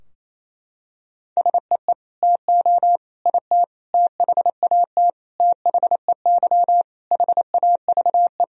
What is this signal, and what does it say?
Morse code, heard as a keyed tone, about 28 wpm, 700 Hz. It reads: SEE TO IT THAT THEY HAVE
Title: Sentences from Top 1000 Words - Review Part 2 - 28wpm